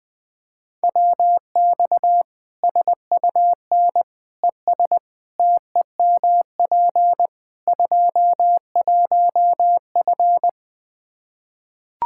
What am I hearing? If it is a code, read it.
Morse code, heard as a keyed tone, about 20 wpm, 700 Hz. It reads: WX SUN ES TEMP 21F
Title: Practice QSO - Exchange 2 - Info - 20wpm